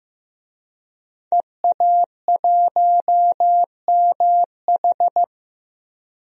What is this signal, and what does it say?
Morse code, heard as a keyed tone, about 15 wpm, 700 Hz. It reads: EA1MH
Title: DXCC-100 Call Signs - 5 Characters - Repeated 3x - 15wpm